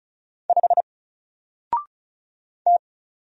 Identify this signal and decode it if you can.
Morse code, heard as a keyed tone, about 36 wpm, 700 Hz. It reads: 5 T
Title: Single Character (Letter:Number:Punctuation:Prosign) Alt 35wpm